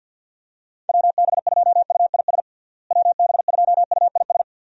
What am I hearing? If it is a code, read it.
Morse code, heard as a keyed tone, about 52 wpm, 700 Hz. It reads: WB2UIS WB2UIS
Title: USA Call Signs - 6 Characters - Repeated 2x - 50wpm